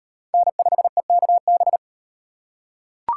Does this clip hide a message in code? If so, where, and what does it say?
Yes: Morse code from 0.3 to 1.8 s, heard as a keyed tone, about 34 wpm, 700 Hz. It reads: N5EKB